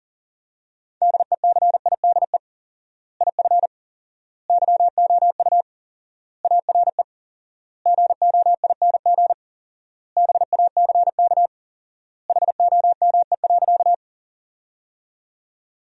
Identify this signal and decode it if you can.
Morse code, heard as a keyed tone, about 40 wpm, 700 Hz. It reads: DECIDE IF YOU ARE GOING BACK HOME.
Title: 4x Word-Spacing - Sentences from Top 500 Words - 40wpm